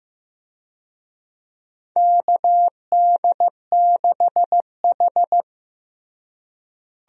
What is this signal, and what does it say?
Morse code, heard as a keyed tone, about 15 wpm, 700 Hz. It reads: KD6H
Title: USA Call Signs - 4 Characters - Repeated 3x - 15wpm